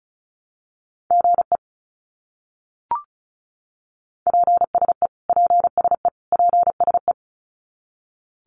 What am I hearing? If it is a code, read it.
Morse code, heard as a keyed tone, about 34 wpm, 700 Hz. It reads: GE PSE PSE PSE
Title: Abbreviations Repeated 3x - 35wpm